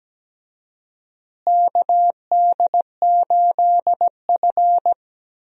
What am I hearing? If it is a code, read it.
Morse code, heard as a keyed tone, about 17 wpm, 700 Hz. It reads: KD8F
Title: USA Call Signs - 4 Characters - Repeated 2x - 17wpm